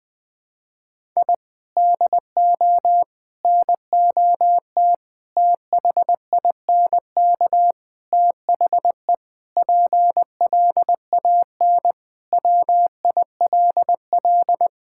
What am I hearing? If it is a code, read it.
Morse code, heard as a keyed tone, about 20 wpm, 700 Hz. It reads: I DO NOT THINK THE PLAN WILL
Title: Sentences with 5-Word Phrases - 20wpm